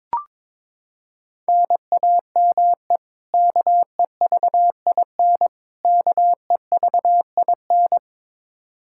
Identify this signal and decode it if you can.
Morse code, heard as a keyed tone, about 22 wpm, 700 Hz. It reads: NAME KEVIN KEVIN
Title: Practice QSO - Exchange 1 - Beginning - 22wpm